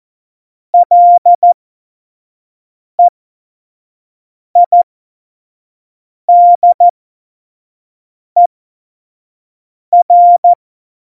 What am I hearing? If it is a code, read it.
Morse code, heard as a keyed tone, about 14 wpm, spaced as for 5 wpm, 700 Hz. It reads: LEIDER
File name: Morsekurs / Witze / Witz 20 March 2018, 25 F70